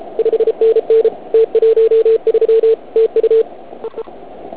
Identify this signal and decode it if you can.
Morse code, heard as a keyed tone, about 35 wpm, 440 Hz. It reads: 5NN T13 TU EE
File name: Archiv / CUC106 / cuc106 PILEUP